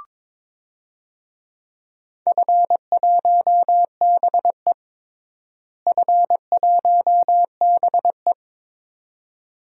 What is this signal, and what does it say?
Morse code, heard as a keyed tone, about 21 wpm, 700 Hz. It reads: F1BE F1BE
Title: DXCC-100 Call Signs - 4 Characters - Repeated 2x - 22wpm